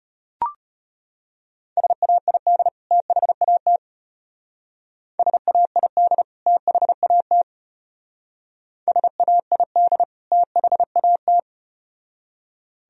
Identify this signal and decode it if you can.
Morse code, heard as a keyed tone, about 32 wpm, 700 Hz. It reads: SAID THAT SAID THAT SAID THAT